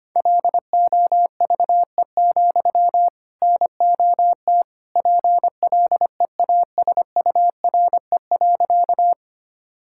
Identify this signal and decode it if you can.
Morse code, heard as a keyed tone, about 25 wpm, 700 Hz. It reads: LOVE, NOT PLEASURE.
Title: Sentences from Top 1600 Words 25wpm